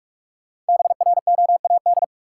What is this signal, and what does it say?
Morse code, heard as a keyed tone, about 46 wpm, 700 Hz. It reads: BROAD
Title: Sentences from Top 1000 Words - New Word Drill - 45wpm